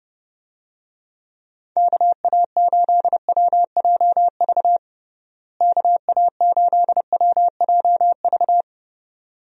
Morse code, heard as a keyed tone, about 30 wpm, 700 Hz. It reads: KA8WJV KA8WJV
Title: USA Call Signs - 6 Characters - Repeated 2x - 30wpm